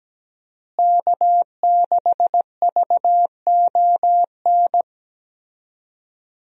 Morse code, heard as a keyed tone, about 17 wpm, 700 Hz. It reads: K6VON
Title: USA Call Signs - 5 Characters - Repeated 3x - 17wpm